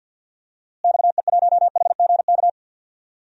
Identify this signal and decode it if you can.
Morse code, heard as a keyed tone, about 51 wpm, 700 Hz. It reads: XE1HGK
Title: DXCC-100 Call Signs - 6 Characters - Repeated 3x - 50wpm